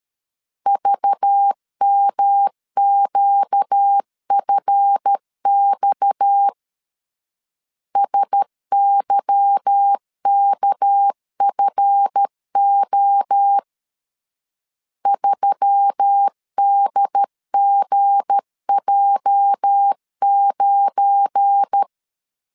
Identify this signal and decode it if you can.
Morse code, heard as a keyed tone, about 13 wpm, 800 Hz. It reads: VMQFX SYKFO 3DGJ9
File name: LEZIONI / 8 YJ45 / 8 YJ 12wpm 2235